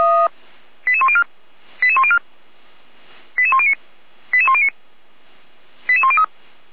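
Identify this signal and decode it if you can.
Five-tone selective call (ZVEI), heard as quick runs of five tones, tones of about 70 ms. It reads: 88104 88104 88109 88109 88103
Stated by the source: Received on SDR as a ZVEI five-tone selective call, code 88569.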